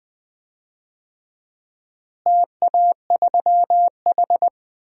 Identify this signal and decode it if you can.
Morse code, heard as a keyed tone, about 20 wpm, 690 Hz. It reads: TA3H
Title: DXCC-100 Call Signs - 4 Characters - Repeated 3x - 20wpm